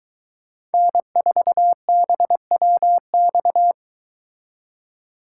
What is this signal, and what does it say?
Morse code, heard as a keyed tone, about 23 wpm, 700 Hz. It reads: N4BWX